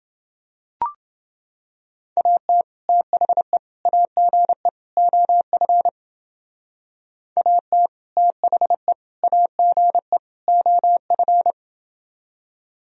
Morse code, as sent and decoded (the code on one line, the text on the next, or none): .- - / - .... / .- --. / --- ..-. / .- - / - .... / .- --. / --- ..-.
AT THE AGE OF AT THE AGE OF